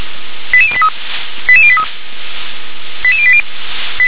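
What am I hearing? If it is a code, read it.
Five-tone selective call (ZVEI), heard as quick runs of five tones, tones of about 70 ms. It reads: 88083 88083 88080
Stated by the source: ZVEI five-tone selective call received on an SDR.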